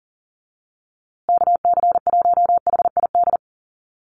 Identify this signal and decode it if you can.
Morse code, heard as a keyed tone, about 40 wpm, 700 Hz. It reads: KC1HID
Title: USA Call Signs Repeated 3x 40wpm